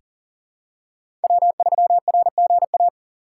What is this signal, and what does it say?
Morse code, heard as a keyed tone, about 40 wpm, 700 Hz. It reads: W3RGA
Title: K1USN SST QSO Practice - 40wpm